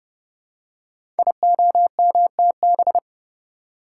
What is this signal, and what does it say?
Morse code, heard as a keyed tone, about 30 wpm, 700 Hz. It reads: IOMTB
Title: Code Groups - 5 Letters - 30wpm